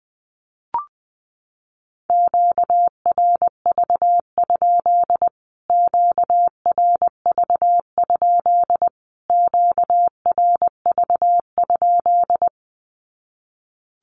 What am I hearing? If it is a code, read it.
Morse code, heard as a keyed tone, about 20 wpm, 700 Hz. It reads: QRV? QRV? QRV?
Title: Q-Codes Repeated 3x 20wpm